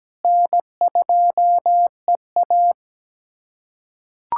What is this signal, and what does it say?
Morse code, heard as a keyed tone, about 17 wpm, 700 Hz. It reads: N2EA
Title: USA Callsign DE Callsign - 4 Characters - 17wpm